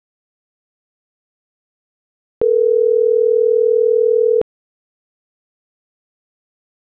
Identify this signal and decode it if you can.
Ringback tone (North America), 440 + 480 Hz.